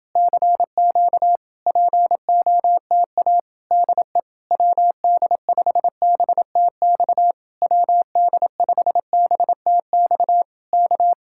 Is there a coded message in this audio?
Yes: Morse code from 0.2 to 11.1 s, heard as a keyed tone, about 30 wpm, 700 Hz. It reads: CQ POTA DE WD5BTX WD5BTX K